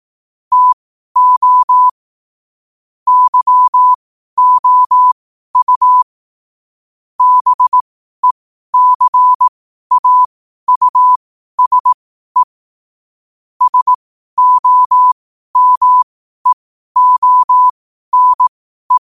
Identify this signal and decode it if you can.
Morse code, heard as a keyed tone, about 18 wpm, spaced as for 12 wpm, 1000 Hz. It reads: TO YOU BECAUSE SOMEONE
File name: 12wpm